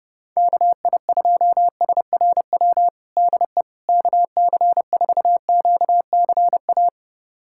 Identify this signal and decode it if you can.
Morse code, heard as a keyed tone, about 30 wpm, 700 Hz. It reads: KI2SRW DE KC4QCA
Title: USA Callsign DE Callsign 30wpm